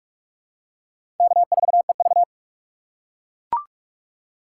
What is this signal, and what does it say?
Morse code, heard as a keyed tone, about 42 wpm, 700 Hz. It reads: K4EV
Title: USA Call Signs - 4 Characters - 3x Character Spacing - 45wpm